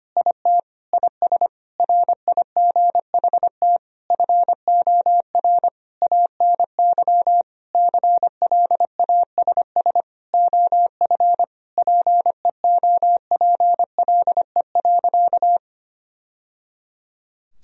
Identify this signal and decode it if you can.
Morse code, heard as a keyed tone, about 25 wpm, 700 Hz. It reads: IT IS RIGHT FOR ANY CLASS OF PEOPLE.
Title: Sentences from Top 1000 Words - Review Part 2 - 25wpm